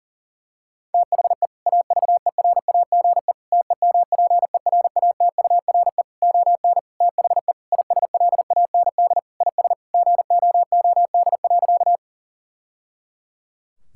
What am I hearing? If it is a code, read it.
Morse code, heard as a keyed tone, about 37 wpm, 700 Hz. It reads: THE AVERAGE TEMPERATURE ON THE ISLAND IS GOOD.
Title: Sentences from Top 1000 Words - Review Part 6 - 40wpm